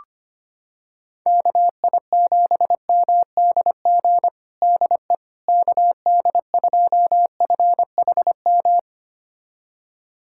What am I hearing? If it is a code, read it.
Morse code, heard as a keyed tone, about 25 wpm, 700 Hz. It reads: KI7MDG DE KD2FHM
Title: USA Callsign DE Callsign 25wpm